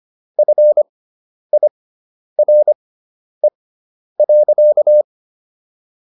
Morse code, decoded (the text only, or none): FIRE.